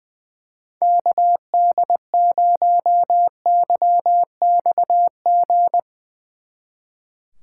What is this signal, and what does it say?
Morse code, heard as a keyed tone, about 24 wpm, 700 Hz. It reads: KD0YXG
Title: Speed Racing - USA 2x3 Call Signs 20wpm